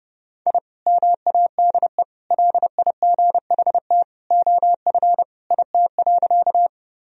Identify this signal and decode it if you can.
Morse code, heard as a keyed tone, about 32 wpm, 700 Hz. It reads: I MADE LIGHT OF IT.
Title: Sentences from Top 1000 Words - Review Part 1 - 30wpm